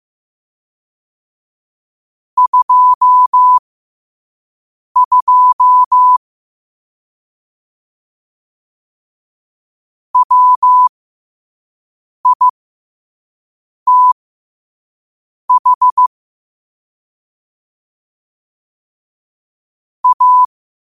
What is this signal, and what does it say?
Morse code, heard as a keyed tone, about 15 wpm, spaced as for 5 wpm, 1000 Hz. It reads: 22 WITH A
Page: New quotes every day in morse code at 5 Words per minute.